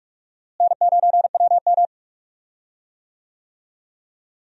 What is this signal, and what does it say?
Morse code, heard as a keyed tone, about 46 wpm, 700 Hz. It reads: N9WM
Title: USA Call Signs - 4 Characters - Repeated 2x - 45wpm